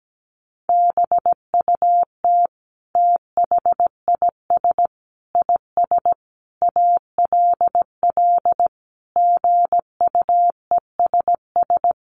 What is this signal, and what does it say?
Morse code, heard as a keyed tone, about 17 wpm, 700 Hz. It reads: BUT THIS IS ALL GUESS
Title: Sentences from Top 1500 Words 17wpm